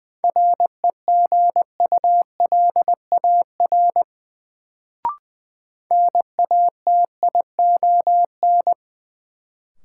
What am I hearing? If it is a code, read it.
Morse code, heard as a keyed tone, about 20 wpm, 700 Hz. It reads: REGULAR NATION
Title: Sentences from Top 900 Words - New Word Drill - 20wpm